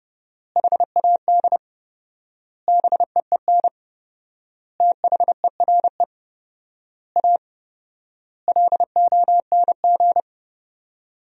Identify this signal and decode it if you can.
Morse code, heard as a keyed tone, about 32 wpm, 700 Hz. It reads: HAD BEEN THERE A LONG
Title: 4x Word-Spacing - Sentences from Top 400 Words - 30wpm